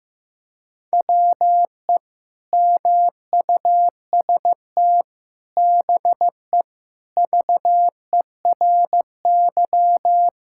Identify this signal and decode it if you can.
Morse code, heard as a keyed tone, about 15 wpm, 700 Hz. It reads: WE MUST BE VERY
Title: Sentences from Top 1600 Words 15wpm